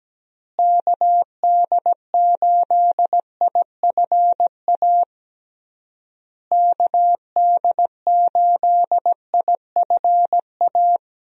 Morse code, heard as a keyed tone, about 17 wpm, 700 Hz. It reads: KD8IFA KD8IFA
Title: USA Call Signs - 6 Characters - Repeated 3x - 17wpm